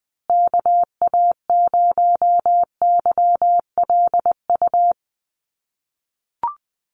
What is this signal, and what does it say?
Morse code, heard as a keyed tone, about 20 wpm, 700 Hz. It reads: KA0YLU
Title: USA Call Signs Repeated 2x 20wpm